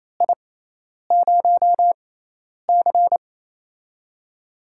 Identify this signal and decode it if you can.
Morse code, heard as a keyed tone, about 27 wpm, spaced as for 10 wpm, 700 Hz. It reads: I0C
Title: USA Call Signs - 4 Characters - 6x Character Spacing - 28wpm